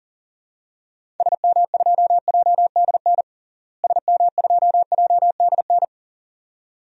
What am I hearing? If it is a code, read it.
Morse code, heard as a keyed tone, about 40 wpm, 700 Hz. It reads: SM2JDN SM2JDN
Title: DXCC-100 Call Signs - 6 Characters - Repeated 2x - 40wpm